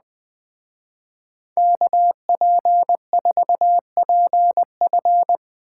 Morse code, heard as a keyed tone, about 20 wpm, 700 Hz. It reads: KP4PF
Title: Speed Racing - USA 2x2 Call Signs 17wpm